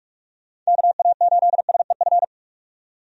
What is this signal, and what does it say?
Morse code, heard as a keyed tone, about 46 wpm, 700 Hz. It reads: KA8SEF
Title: USA Call Signs - 6 Characters - 45wpm